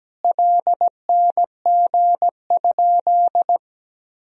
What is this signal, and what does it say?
Morse code, heard as a keyed tone, about 17 wpm, 700 Hz. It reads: LNG?
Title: Sentences from Top 500 Words and Abbreviations 17wpm